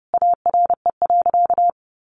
Morse code, heard as a keyed tone, about 30 wpm, 700 Hz. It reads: ARE.
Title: Sentences from Top 1200 Words 30wpm